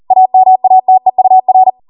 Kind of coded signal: Morse code